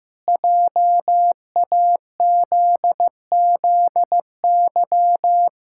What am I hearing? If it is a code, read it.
Morse code, heard as a keyed tone, about 15 wpm, 700 Hz. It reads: JAZZY
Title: Pangram Challenge v2 - 15wpm